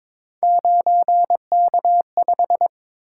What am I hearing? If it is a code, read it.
Morse code, heard as a keyed tone, about 22 wpm, 700 Hz. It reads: 9K5
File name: practice / 22wpm / Three Letter-Number Combinations - 22wpm